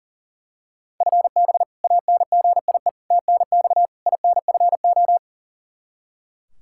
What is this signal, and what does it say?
Morse code, heard as a keyed tone, about 40 wpm, 700 Hz. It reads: FB ANGIE TNX INFO
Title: K1USN SST QSO Practice 40wpm